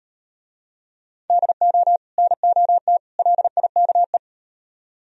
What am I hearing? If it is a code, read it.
Morse code, heard as a keyed tone, about 36 wpm, 700 Hz. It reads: DO NOT LIKE